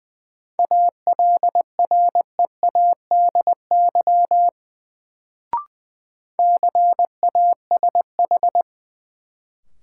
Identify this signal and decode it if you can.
Morse code, heard as a keyed tone, about 20 wpm, 700 Hz. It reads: ALREADY CASH
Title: Sentences from Top 700 Words - New Word Drill - 20wpm